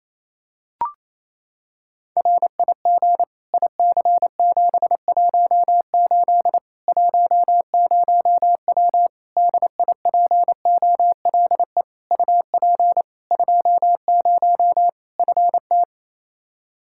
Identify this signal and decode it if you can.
Morse code, heard as a keyed tone, about 28 wpm, 700 Hz. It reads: RIG IC718 10W DIPOLE UP 20 FT
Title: Practice QSO - Exchange 2 - Info - 28wpm